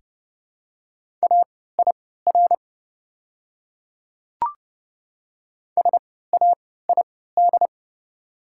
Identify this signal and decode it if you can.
Morse code, heard as a keyed tone, about 31 wpm, spaced as for 17 wpm, 700 Hz. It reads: AIR SAID
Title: 3x Character Spacing - Top 200 Words - 30wpm